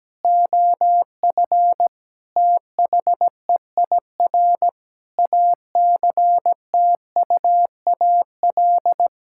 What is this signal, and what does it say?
Morse code, heard as a keyed tone, about 17 wpm, 700 Hz. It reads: OF THEIR ACTUAL
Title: Sentences from Top 1000 Words - Review Part 5 - 17wpm